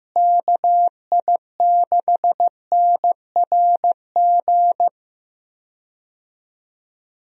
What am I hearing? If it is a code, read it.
Morse code, heard as a keyed tone, about 15 wpm, 700 Hz. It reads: KI6NRG
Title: USA Call Signs - 6 Characters - Repeated 3x - 15wpm